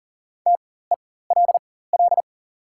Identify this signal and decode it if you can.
Morse code, heard as a keyed tone, about 38 wpm, spaced as for 18 wpm, 690 Hz. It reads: TELL
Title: 4x Character Spacing - Top 500 Words - 40wpm